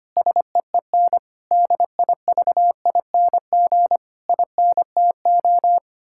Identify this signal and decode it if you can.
Morse code, heard as a keyed tone, about 25 wpm, 700 Hz. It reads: SEEN DIVING INTO
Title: Pangram Challenge v2 - 25wpm